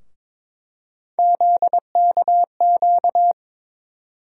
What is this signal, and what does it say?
Morse code, heard as a keyed tone, about 22 wpm, 700 Hz. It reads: ZKQ
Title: USA Call Sign Postfixes - 3 Letters 22wpm